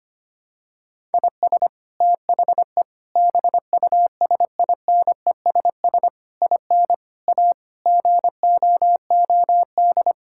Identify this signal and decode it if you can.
Morse code, heard as a keyed tone, about 25 wpm, 700 Hz. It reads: IS THE BUSINESS IN A GOOD
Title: Sentences from Top 1000 Words - Review Part 5 - 25wpm